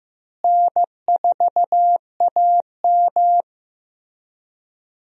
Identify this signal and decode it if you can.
Morse code, heard as a keyed tone, about 15 wpm, 700 Hz. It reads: N4AM
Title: USA Call Signs - 4 Characters - Repeated 3x - 15wpm